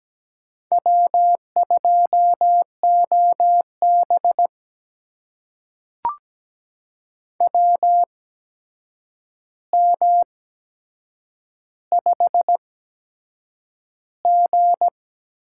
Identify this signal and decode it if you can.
Morse code, heard as a keyed tone, about 17 wpm, spaced as for 7 wpm, 700 Hz. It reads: W2OB WM5G
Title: USA Call Signs - 4 Characters - 8x Character Spacing - 17wpm